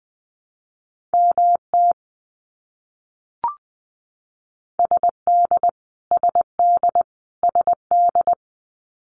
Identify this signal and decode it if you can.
Morse code, heard as a keyed tone, about 20 wpm, 700 Hz. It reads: MT SD SD SD
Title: US States Repeated 3x 20wpm